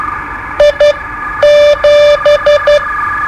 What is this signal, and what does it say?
Morse code, heard as a keyed tone, about 12 wpm, spaced as for 9 wpm, 590 Hz. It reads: I7
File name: qso-01